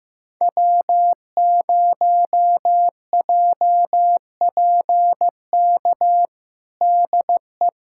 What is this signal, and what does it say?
Morse code, heard as a keyed tone, about 15 wpm, 700 Hz. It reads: W0JPK DE
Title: USA Callsign DE Callsign - 5 Characters - 15wpm